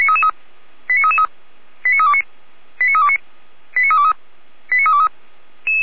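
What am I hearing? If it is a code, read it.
Five-tone selective call (ZVEI), heard as quick runs of five tones, tones of about 70 ms. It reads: 89333 89329 89329 89323 89323
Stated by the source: ZVEI five-tone call 89508 received by SDR at feuerwehr-orlamuende (fire brigade Orlamünde).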